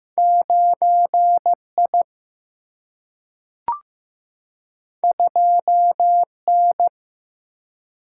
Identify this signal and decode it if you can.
Morse code, heard as a keyed tone, about 16 wpm, 700 Hz. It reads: 9I 2N
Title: Single-Letter Single-Number 15wpm